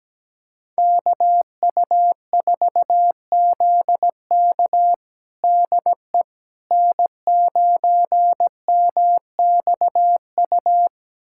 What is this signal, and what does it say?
Morse code, heard as a keyed tone, about 17 wpm, 700 Hz. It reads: KU4ZK DE N9MXU
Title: USA Callsign DE Callsign - 5 Characters - 17wpm